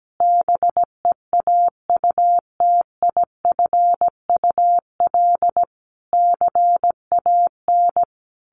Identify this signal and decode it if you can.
Morse code, heard as a keyed tone, about 17 wpm, 700 Hz. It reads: BEAUTIFUL CAN
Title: Sentences from Top 1500 Words 17wpm